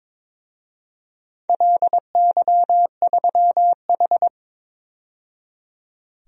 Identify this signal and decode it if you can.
Morse code, heard as a keyed tone, about 22 wpm, 700 Hz. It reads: LY3H
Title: DXCC-100 Call Signs - 4 Characters - 22wpm